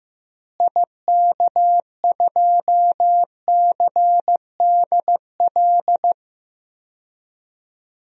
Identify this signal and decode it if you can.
Morse code, heard as a keyed tone, about 15 wpm, 700 Hz. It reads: IK2CDL
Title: DXCC-100 Call Signs - 6 Characters - Repeated 3x - 15wpm